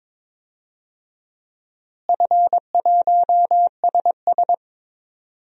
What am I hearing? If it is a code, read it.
Morse code, heard as a keyed tone, about 22 wpm, 700 Hz. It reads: F1SS
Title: DXCC-100 Call Signs - 4 Characters - Repeated 2x - 22wpm